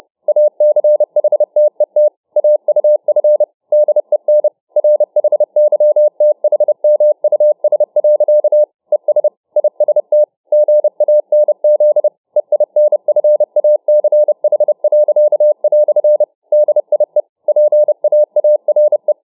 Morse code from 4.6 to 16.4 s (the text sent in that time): RHYTHMUS. ES IST GANZ EINFACH."